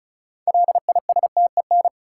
Morse code, heard as a keyed tone, about 34 wpm, 700 Hz. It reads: LISTEN
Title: Sentences from Top 400 Words - New Word Drill - 35wpm